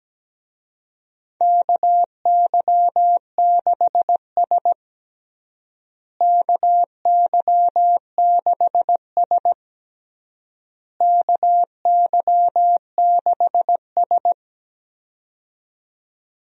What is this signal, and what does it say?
Morse code, heard as a keyed tone, about 17 wpm, 700 Hz. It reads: KY6S KY6S KY6S
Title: USA Call Signs - 4 Characters - Repeated 3x - 17wpm